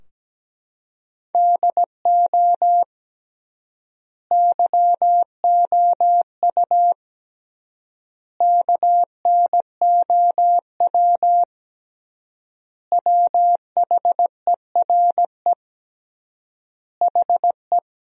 Morse code, heard as a keyed tone, about 17 wpm, 700 Hz. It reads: DO YOU KNOW WHERE HE
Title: 3x Word-Spacing - Sentences from Top 300 Words - 17wpm